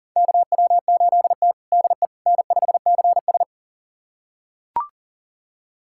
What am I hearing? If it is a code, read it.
Morse code, heard as a keyed tone, about 37 wpm, 700 Hz. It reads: KW8T DE N5CS
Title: USA Callsign DE Callsign - 4 Characters - 40wpm